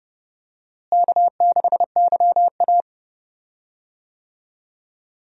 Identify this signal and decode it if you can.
Morse code, heard as a keyed tone, about 30 wpm, 700 Hz. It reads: K6YA
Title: USA Call Signs - 4 Characters - Repeated 2x - 30wpm